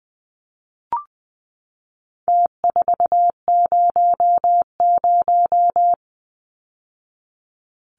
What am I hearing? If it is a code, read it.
Morse code, heard as a keyed tone, about 20 wpm, 700 Hz. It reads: T400
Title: Numbers - Contest Style - 4-Digit 20wpm